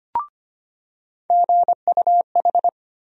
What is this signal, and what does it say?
Morse code, heard as a keyed tone, about 25 wpm, 700 Hz. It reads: GUH
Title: USA Call Sign Postfixes - 3 Letters 25wpm